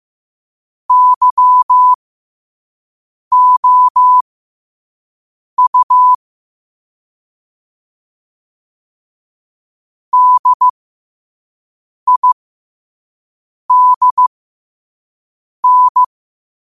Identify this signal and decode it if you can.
Morse code, heard as a keyed tone, about 15 wpm, spaced as for 5 wpm, 1000 Hz. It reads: YOU DIDN
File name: qotdfiles / 5wpm